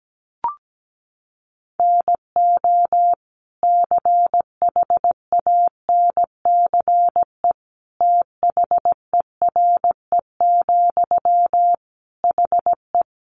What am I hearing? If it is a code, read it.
Morse code, heard as a keyed tone, about 17 wpm, 700 Hz. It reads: NO CHANCE THERE, HE
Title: Sentences from Top 800 Words 17wpm